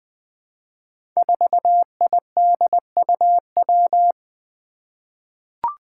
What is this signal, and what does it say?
Morse code, heard as a keyed tone, about 20 wpm, 700 Hz. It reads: 4IDUW
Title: Five Challenging Letter-Number Combinations 20wpm